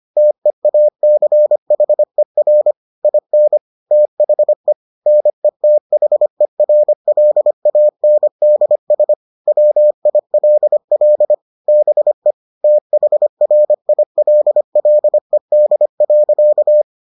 Morse code, heard as a keyed tone, about 25 wpm, 600 Hz. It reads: TEACHER IN THE NETHERLANDS WILL BE THRILLED.